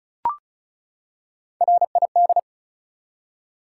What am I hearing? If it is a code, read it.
Morse code, heard as a keyed tone, about 34 wpm, 700 Hz. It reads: RID
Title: Three Letter Words - 35wpm